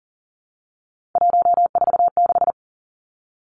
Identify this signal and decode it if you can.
Morse code, heard as a keyed tone, about 38 wpm, 700 Hz. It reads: 146
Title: Three Challenging Letter-Number Combinations 40wpm